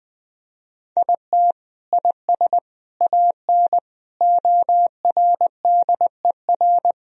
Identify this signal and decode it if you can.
Morse code, heard as a keyed tone, about 20 wpm, 700 Hz. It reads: IT IS AN ORDER
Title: Sentences from Top 1000 Words - Review Part 2 - 20wpm